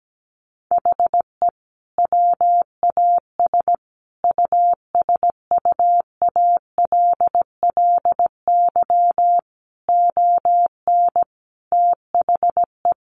Morse code, heard as a keyed tone, about 17 wpm, 700 Hz. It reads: HE WAS USUALLY ON THE
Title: Sentences from Top 1400 Words 17wpm